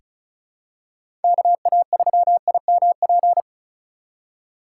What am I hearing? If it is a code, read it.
Morse code, heard as a keyed tone, about 36 wpm, 690 Hz. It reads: KA3IMP
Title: USA Call Signs - 6 Characters - 6x Character Spacing - 35wpm